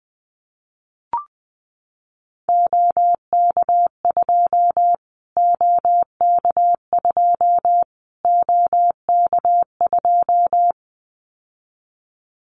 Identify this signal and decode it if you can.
Morse code, heard as a keyed tone, about 20 wpm, 700 Hz. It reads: OK2 OK2 OK2
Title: Callsign Prefixes for Top 25 DXCC - Repeated 3x - 20wpm